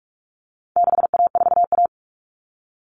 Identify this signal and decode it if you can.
Morse code, heard as a keyed tone, about 44 wpm, 700 Hz. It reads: 6A4A
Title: Common Cut Numbers - 4-Digit 45wpm